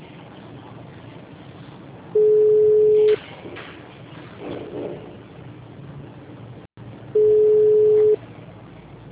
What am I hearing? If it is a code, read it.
Ringback tone (Europe, 425 Hz).